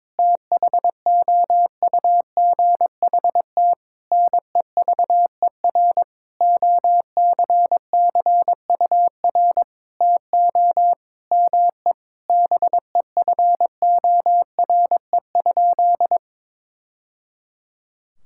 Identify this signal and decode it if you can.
Morse code, heard as a keyed tone, about 22 wpm, 700 Hz. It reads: THOUGHT NEVER OCCUR TO ME BEFORE?
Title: Sentences from Top 1000 Words - Review Part 6 - 22wpm